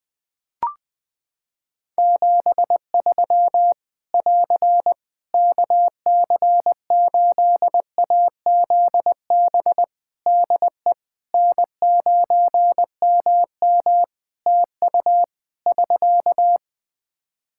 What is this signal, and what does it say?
Morse code, heard as a keyed tone, about 20 wpm, 700 Hz. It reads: 73 <AR> KC8AZB DE N9MM TU <SK>
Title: K1USN SST QSO Practice 20wpm